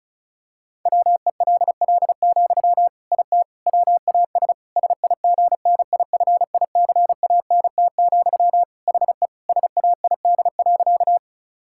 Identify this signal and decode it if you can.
Morse code, heard as a keyed tone, about 36 wpm, 700 Hz. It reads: WELL, IT WAS SIGNIFICANT, HE SAID.
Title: Sentences from Top 1000 Words - Review Part 6 - 35wpm